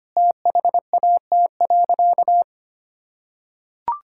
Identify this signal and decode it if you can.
Morse code, heard as a keyed tone, about 25 wpm, 700 Hz. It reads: THAT.
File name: practice / 25wpm / Sentences with 2-Word Phrases - 25wpm